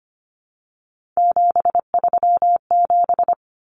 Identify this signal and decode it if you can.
Morse code, heard as a keyed tone, about 25 wpm, 700 Hz. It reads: 737
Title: Common Cut Numbers - 3-Digit 25wpm